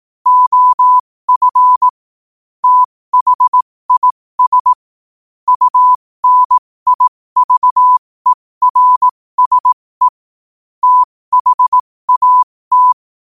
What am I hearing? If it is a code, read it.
Morse code, heard as a keyed tone, about 18 wpm, spaced as for 15 wpm, 1000 Hz. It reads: OF THIS UNIVERSE THAT
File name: qotdfiles / 15wpm